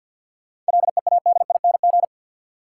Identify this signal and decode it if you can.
Morse code, heard as a keyed tone, about 52 wpm, 700 Hz. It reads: LEADING
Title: Sentences from Top 1100 Words - New Word Drill Alt - 50wpm